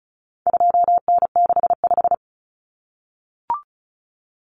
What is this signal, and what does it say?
Morse code, heard as a keyed tone, about 34 wpm, 700 Hz. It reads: 2N65
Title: Common Cut Numbers - 4-Digit 35wpm